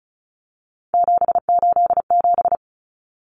Morse code, heard as a keyed tone, about 34 wpm, 700 Hz. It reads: 787
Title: Common Cut Numbers - 3-Digit 35wpm